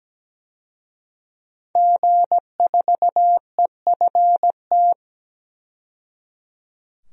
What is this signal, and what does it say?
Morse code, heard as a keyed tone, about 17 wpm, 700 Hz. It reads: G4EFT
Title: DXCC-100 Call Signs - 5 Characters - 17wpm